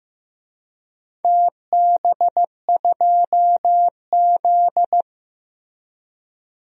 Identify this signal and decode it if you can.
Morse code, heard as a keyed tone, about 15 wpm, 700 Hz. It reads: TB2Z
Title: DXCC-100 Call Signs - 4 Characters - Repeated 2x - 15wpm